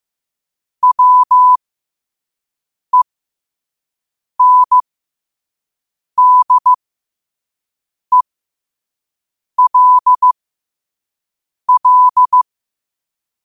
Morse code, heard as a keyed tone, about 15 wpm, spaced as for 5 wpm, 1000 Hz. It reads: WENDELL
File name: qotdfiles / 5wpm